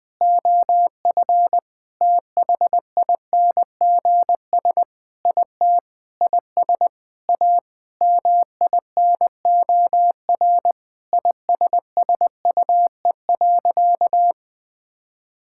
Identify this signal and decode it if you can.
Morse code, heard as a keyed tone, about 20 wpm, 700 Hz. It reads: OF THINGS IT IS A MINOR ISSUE.
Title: Sentences with 6-Word Phrases - 20wpm